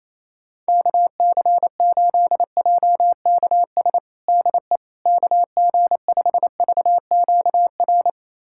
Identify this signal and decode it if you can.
Morse code, heard as a keyed tone, about 28 wpm, 700 Hz. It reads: KC8JKS DE KG5VQR
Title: USA Callsign DE Callsign - 6 Characters - 28wpm